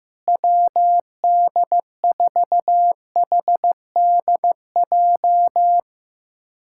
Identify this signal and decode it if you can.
Morse code, heard as a keyed tone, about 15 wpm, 700 Hz. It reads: WD4HDJ
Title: USA Callsign DE Callsign - 6 Characters - 15wpm